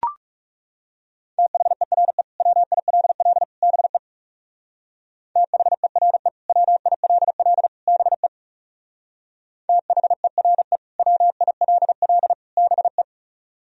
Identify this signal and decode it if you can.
Morse code, heard as a keyed tone, about 40 wpm, 700 Hz. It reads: THERE WILL BE THERE WILL BE THERE WILL BE